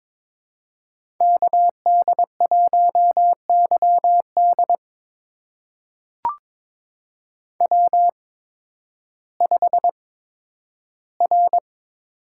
Morse code, heard as a keyed tone, about 22 wpm, spaced as for 12 wpm, 700 Hz. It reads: KD1YD W5R
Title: USA Call Signs - 5 Characters - 8x Character Spacing - 22wpm